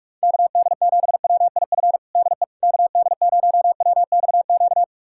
Morse code, heard as a keyed tone, about 42 wpm, 700 Hz. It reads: KD7WIF DE KD0WXQ
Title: USA Callsign DE Callsign - 6 Characters - 45wpm